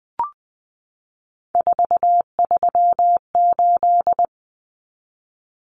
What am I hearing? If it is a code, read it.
Morse code, heard as a keyed tone, about 20 wpm, 700 Hz. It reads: 438
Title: Common Cut Numbers - 3-Digit 20wpm